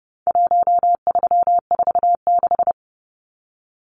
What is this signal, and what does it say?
Morse code, heard as a keyed tone, about 29 wpm, 700 Hz. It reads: 1346
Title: Numbers - Contest Style - 4-Digit 30wpm